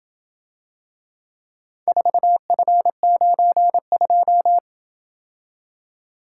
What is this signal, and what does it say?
Morse code, heard as a keyed tone, about 27 wpm, 700 Hz. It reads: 4F92